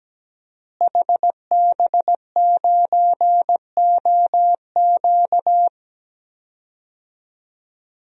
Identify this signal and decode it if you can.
Morse code, heard as a keyed tone, about 17 wpm, 700 Hz. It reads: HB9OQ
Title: DXCC-100 Call Signs - 5 Characters - Repeated 3x - 17wpm